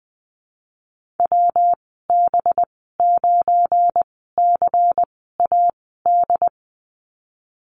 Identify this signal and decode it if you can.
Morse code, heard as a keyed tone, about 19 wpm, spaced as for 14 wpm, 700 Hz. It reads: WB9CAD
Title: 193 Lesson 31g - Letter v - Call Signs 20wpm